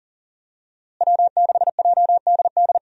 Morse code, heard as a keyed tone, about 40 wpm, 700 Hz. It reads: W6JDD
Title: USA Call Signs - 5 Characters - Repeated 2x - 40wpm